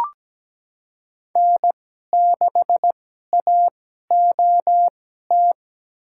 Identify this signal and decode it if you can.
Morse code, heard as a keyed tone, about 17 wpm, spaced as for 12 wpm, 700 Hz. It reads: N6AOT
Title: USA Call Signs - 5 Characters - 2x Character Spacing - 17wpm